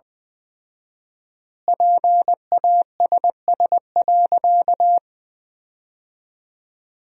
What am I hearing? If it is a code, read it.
Morse code, heard as a keyed tone, about 21 wpm, 700 Hz. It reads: PASS.
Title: 4x Word-Spacing - Sentences from Top 400 Words - 20wpm